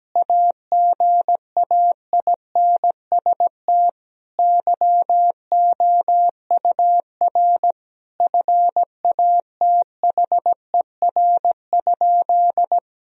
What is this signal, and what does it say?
Morse code, heard as a keyed tone, about 17 wpm, 700 Hz. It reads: AGAINST YOUR FATHER?
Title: Sentences from Top 1000 Words - Review Part 3 - 17wpm